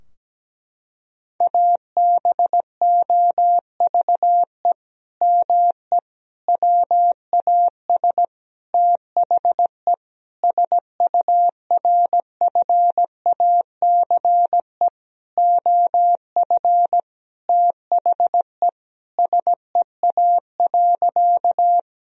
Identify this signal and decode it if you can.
Morse code, heard as a keyed tone, about 17 wpm, 700 Hz. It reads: ABOVE ME WAS THE SURFACE OF THE SEA.
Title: Sentences from Top 1000 Words - Review Part 3 - 17wpm